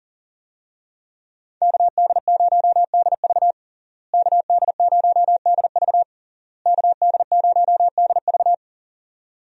Morse code, heard as a keyed tone, about 40 wpm, 700 Hz. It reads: KD0DV KD0DV KD0DV
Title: USA Call Signs - 5 Characters - Repeated 3x - 40wpm